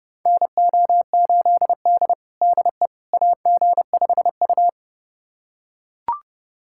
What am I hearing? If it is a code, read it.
Morse code, heard as a keyed tone, about 30 wpm, 700 Hz. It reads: NO8D DE AG5U
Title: USA Callsign DE Callsign - 4 Characters - 30wpm